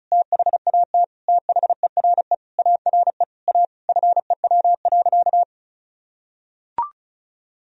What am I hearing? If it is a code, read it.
Morse code, heard as a keyed tone, about 34 wpm, 700 Hz. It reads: THAT THERE ARE A FEW.
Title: Sentences from Top 1000 Words - Review Part 1 - 35wpm